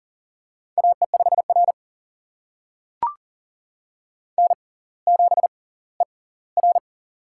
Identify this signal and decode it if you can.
Morse code, heard as a keyed tone, about 39 wpm, spaced as for 20 wpm, 700 Hz. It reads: AE5R N7ER
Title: USA Call Signs - 4 Characters - 6x Character Spacing - 40wpm